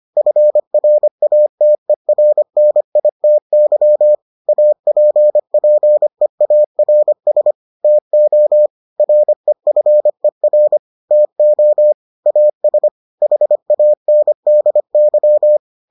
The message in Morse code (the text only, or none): FRATERNITY APPEARS TO REFER TO AS HANDY